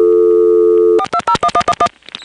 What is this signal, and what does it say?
DTMF, heard as telephone touch tones, tones of about 70 ms. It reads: 4201111
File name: phonetrips / SmyrnaGA5HQ